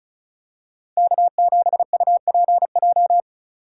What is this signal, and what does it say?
Morse code, heard as a keyed tone, about 36 wpm, 700 Hz. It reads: K7UPJ
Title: K1USN SST QSO Practice - 35wpm